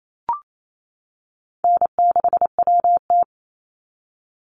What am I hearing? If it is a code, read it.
Morse code, heard as a keyed tone, about 28 wpm, 700 Hz. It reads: N6WT
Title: NA SOTA Chaser Callsigns 28wpm